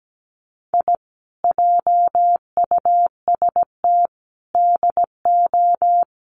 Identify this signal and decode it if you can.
Morse code, heard as a keyed tone, about 17 wpm, 700 Hz. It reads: I JUST DO
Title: Sets of 5 Words Encore 17wpm